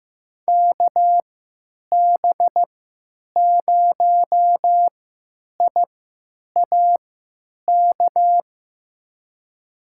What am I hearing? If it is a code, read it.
Morse code, heard as a keyed tone, about 16 wpm, spaced as for 9 wpm, 700 Hz. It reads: KB0IAK
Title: USA Call Signs - 6 Characters - 3x Character Spacing - 15wpm